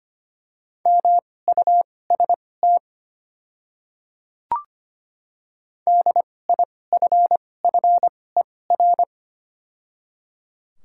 Morse code, heard as a keyed tone, about 25 wpm, spaced as for 18 wpm, 690 Hz. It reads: MUST DIFFER
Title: 2x Character Spacing - Top 200 Words - 25wpm